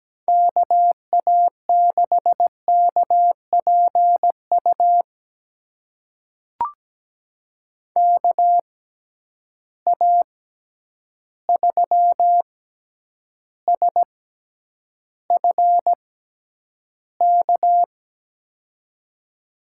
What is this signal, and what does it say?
Morse code, heard as a keyed tone, about 17 wpm, spaced as for 9 wpm, 700 Hz. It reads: KA6KPU KA3SFK
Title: USA Call Signs - 6 Characters - 6x Character Spacing - 17wpm